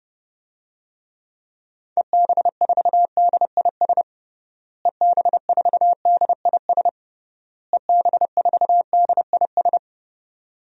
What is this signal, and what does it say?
Morse code, heard as a keyed tone, about 32 wpm, 700 Hz. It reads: EB4DIS EB4DIS EB4DIS
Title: DXCC-100 Call Signs - 6 Characters - Repeated 3x - 30wpm